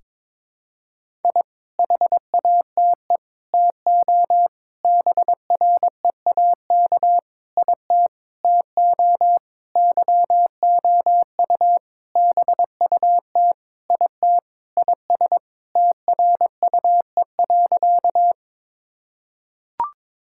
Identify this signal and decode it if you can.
Morse code, heard as a keyed tone, about 22 wpm, 700 Hz. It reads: I HATE TO BREAK IT TO YOU BUT IT IS TRUE.
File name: practice / 22wpm / Sentences with 8-Word Phrases - 22wpm